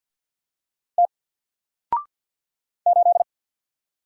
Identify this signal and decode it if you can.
Morse code, heard as a keyed tone, about 48 wpm, 700 Hz. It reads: T 8
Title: Single Character (Letter:Number:Punctuation:Prosign) 50wpm